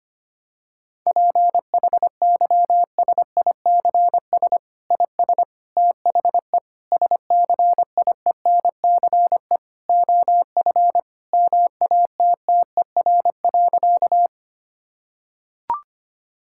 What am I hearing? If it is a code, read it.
Morse code, heard as a keyed tone, about 25 wpm, 700 Hz. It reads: PHYSICS IS THE SCIENCE OF MATTER.
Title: Sentences from Top 1000 Words - Review Part 5 - 25wpm